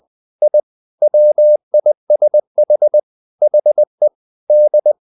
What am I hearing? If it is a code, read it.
Morse code, heard as a keyed tone, about 20 wpm, 600 Hz. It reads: I WISH HE D
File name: the old man and the sea / the old man and the sea part 1 20wpm